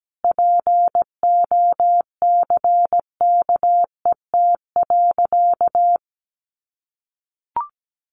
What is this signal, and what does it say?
Morse code, heard as a keyed tone, about 17 wpm, 700 Hz. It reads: POCKET.
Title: A Sound Of Thunder 17wpm